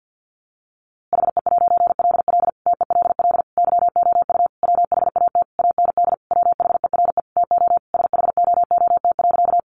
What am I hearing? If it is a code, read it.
Morse code, heard as a keyed tone, about 50 wpm, 700 Hz. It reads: HE'LL TELL YOU WHAT AND WHERE TO SHOOT.
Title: A Sound Of Thunder 50wpm